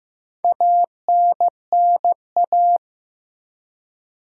Morse code, heard as a keyed tone, about 15 wpm, 700 Hz. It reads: ANNA
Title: USA Common Names - 15wpm